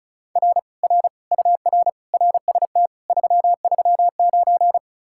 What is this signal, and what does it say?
Morse code, heard as a keyed tone, about 48 wpm, spaced as for 42 wpm, 700 Hz. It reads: R R UR RST 339
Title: Speed-Racing Common QSO Phrases - 35wpm